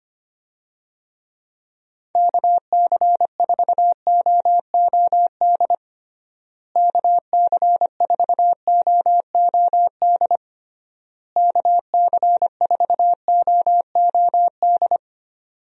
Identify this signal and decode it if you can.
Morse code, heard as a keyed tone, about 25 wpm, 700 Hz. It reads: KC4OOD KC4OOD KC4OOD
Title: USA Call Signs - 6 Characters - Repeated 3x - 25wpm